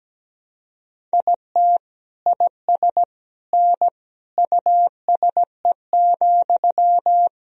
Morse code, heard as a keyed tone, about 17 wpm, 700 Hz. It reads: IT IS N USE,
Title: Sentences from Top 500 Words and Abbreviations 17wpm